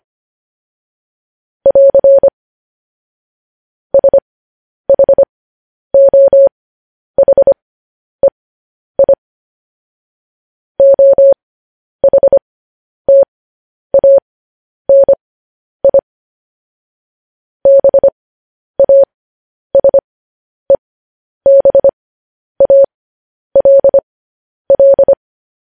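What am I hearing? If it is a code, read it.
Morse code, heard as a keyed tone, about 25 wpm, spaced as for 10 wpm, 550 Hz. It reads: <AR> SHOHEI OHTANI BASEBALL
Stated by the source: Morse code news headlines at 10 wpm.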